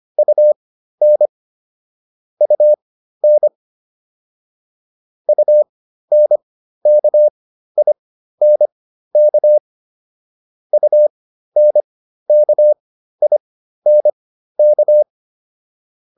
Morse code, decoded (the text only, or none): UN UN UNKINK UNKINK